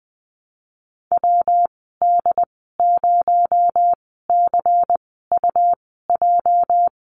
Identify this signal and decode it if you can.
Morse code, heard as a keyed tone, about 20 wpm, spaced as for 15 wpm, 700 Hz. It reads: WD0CUJ
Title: 238 Lesson 38g - Letter q - Call Signs 20wpm